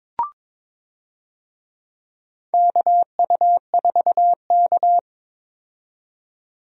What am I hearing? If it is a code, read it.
Morse code, heard as a keyed tone, about 21 wpm, 700 Hz. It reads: KU4K
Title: USA Call Signs - 4 Characters - 22wpm